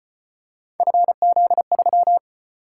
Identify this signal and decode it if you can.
Morse code, heard as a keyed tone, about 35 wpm, 700 Hz. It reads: FZ3